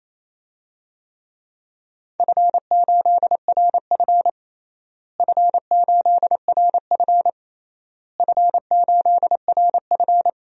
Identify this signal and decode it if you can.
Morse code, heard as a keyed tone, about 28 wpm, 700 Hz. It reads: F8RF F8RF F8RF
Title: DXCC-100 Call Signs - 4 Characters - Repeated 3x - 28wpm